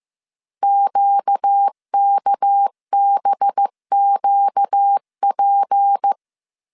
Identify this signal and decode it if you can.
Morse code, heard as a keyed tone, about 15 wpm, 800 Hz. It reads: QKBQP
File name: CW / DET / 1 6 1500C S C 14wpm 2951